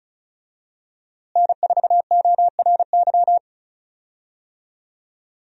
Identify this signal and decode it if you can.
Morse code, heard as a keyed tone, about 36 wpm, 700 Hz. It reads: N4ORY
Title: USA Call Signs - 5 Characters - Repeated 2x - 35wpm